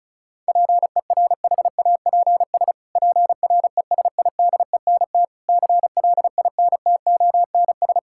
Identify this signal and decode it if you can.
Morse code, heard as a keyed tone, about 36 wpm, 700 Hz. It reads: PERHAPS PRESIDENT CLINTONS
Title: Pangram Challenge v2 - 35wpm